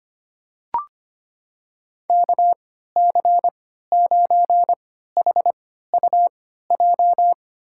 Morse code, heard as a keyed tone, about 25 wpm, spaced as for 14 wpm, 700 Hz. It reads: KC9HUJ
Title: USA Call Signs - 6 Characters - 3x Character Spacing - 25wpm